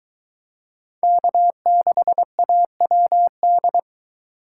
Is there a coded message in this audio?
Yes: Morse code from 1.0 to 3.8 s, heard as a keyed tone, about 22 wpm, 700 Hz. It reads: K6AWD